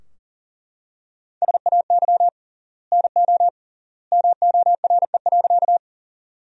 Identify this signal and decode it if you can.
Morse code, heard as a keyed tone, about 40 wpm, 700 Hz. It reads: SAY NO MORE.
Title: 3x Word-Spacing - Sentences from Top 200 Words - 40wpm